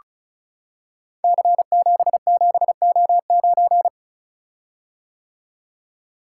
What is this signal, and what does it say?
Morse code, heard as a keyed tone, about 36 wpm, 700 Hz. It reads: C77O9
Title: Code Groups - 5 Letter:Numbers - 35wpm